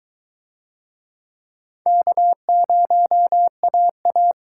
Morse code, heard as a keyed tone, about 23 wpm, 700 Hz. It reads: K0AA